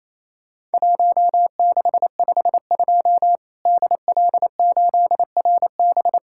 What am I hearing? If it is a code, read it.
Morse code, heard as a keyed tone, about 28 wpm, 700 Hz. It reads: 1652 DL8RB
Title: Worked All Europe DX Contest - QTC Exchange - 28wpm